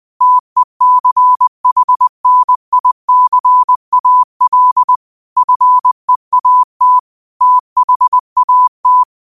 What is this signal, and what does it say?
Morse code, heard as a keyed tone, about 20 wpm, 1000 Hz. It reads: TECHNICAL FEAT THAT
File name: qotdfiles / 20wpm